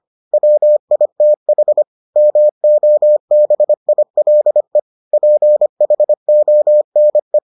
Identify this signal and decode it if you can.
Morse code, heard as a keyed tone, about 25 wpm, 600 Hz. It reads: WITH MOBILE PHONE